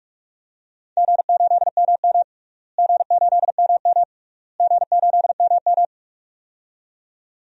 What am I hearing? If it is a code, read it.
Morse code, heard as a keyed tone, about 46 wpm, 700 Hz. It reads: G8MM G8MM G8MM
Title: DXCC-100 Call Signs - 4 Characters - Repeated 3x - 45wpm